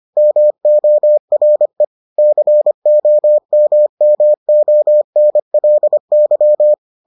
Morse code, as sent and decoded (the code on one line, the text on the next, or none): -- --- .-. / -.-. --- -- -- --- -. .-.. -.--
MORE COMMONLY